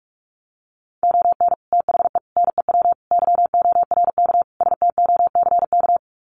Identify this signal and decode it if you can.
Morse code, heard as a keyed tone, about 44 wpm, 700 Hz. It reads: ON THE NEW YORK STOCK
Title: Sets of 5 Words Encore 45wpm